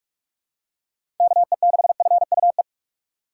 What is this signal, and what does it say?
Morse code, heard as a keyed tone, about 45 wpm, 700 Hz. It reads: KE6FUE
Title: USA Call Signs - 6 Characters - 2x Character Spacing - 45wpm